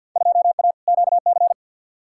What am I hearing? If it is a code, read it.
Morse code, heard as a keyed tone, about 52 wpm, 700 Hz. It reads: 2A QC
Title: Field Day Exchanges - 50wpm